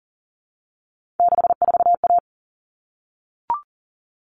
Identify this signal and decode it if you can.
Morse code, heard as a keyed tone, about 40 wpm, 700 Hz. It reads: 64A
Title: All Cut Numbers - 3-Digit 40wpm